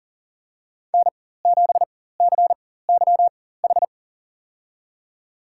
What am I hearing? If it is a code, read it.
Morse code, heard as a keyed tone, about 40 wpm, spaced as for 19 wpm, 700 Hz. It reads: N7CYH
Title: USA Call Signs - 4x Character Spacing - 40wpm